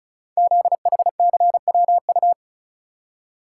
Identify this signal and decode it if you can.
Morse code, heard as a keyed tone, about 36 wpm, 700 Hz. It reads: ZHCWU
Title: Code Groups - 5 Letters - 35wpm